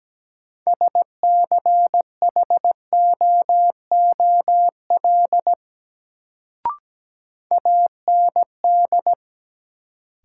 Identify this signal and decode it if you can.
Morse code, heard as a keyed tone, about 17 wpm, 700 Hz. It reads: SCHOOL AND
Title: Sets of 2 Words - New Word Drill - 17wpm